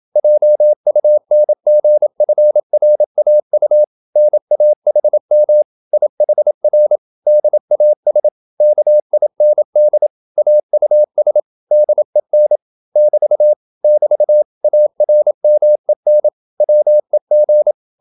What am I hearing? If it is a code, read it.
Morse code, heard as a keyed tone, about 27 wpm, 600 Hz. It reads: JUNGFRAU NAHM IHR DAS KIND AUS DEN <BT> <BT> ARMEN WEG